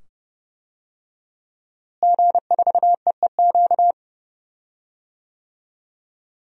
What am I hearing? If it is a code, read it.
Morse code, heard as a keyed tone, about 30 wpm, 700 Hz. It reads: G4EEQ
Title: DXCC-100 Call Signs - 5 Characters - Repeated 2x - 30wpm